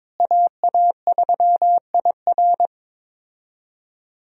Morse code, heard as a keyed tone, about 22 wpm, 700 Hz. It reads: AA3IR
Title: USA Call Signs - 5 Characters - Repeated 3x - 22wpm